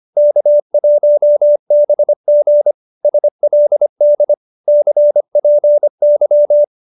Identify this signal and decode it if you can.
Morse code, heard as a keyed tone, about 25 wpm, 570 Hz. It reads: K1BG SLD CPY